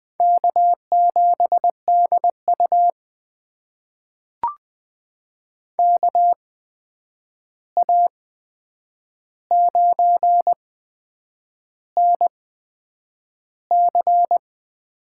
Morse code, heard as a keyed tone, about 20 wpm, spaced as for 8 wpm, 700 Hz. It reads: K7DU KA9NC
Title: USA Call Signs - 8x Character Spacing - 20wpm